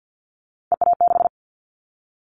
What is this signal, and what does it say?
Morse code, heard as a keyed tone, about 50 wpm, 700 Hz. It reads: EA6
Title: Callsign Prefixes for Top 100 DXCC - 50wpm